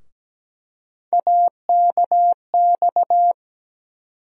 Morse code, heard as a keyed tone, about 17 wpm, 700 Hz. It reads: AKX
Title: USA Call Sign Postfixes - 3 Letters 17wpm